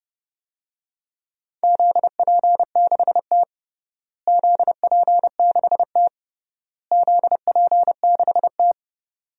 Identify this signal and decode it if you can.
Morse code, heard as a keyed tone, about 30 wpm, 700 Hz. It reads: ZP6T ZP6T ZP6T
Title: DXCC-100 Call Signs - 4 Characters - Repeated 3x - 30wpm